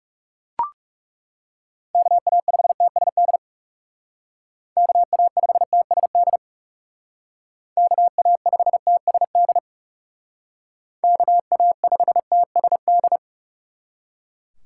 Morse code, heard as a keyed tone, about 35 wpm, 700 Hz. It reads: KA5TSD KA5TSD KA5TSD KA5TSD
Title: Speed Racing - USA 2x3 Call Signs 30wpm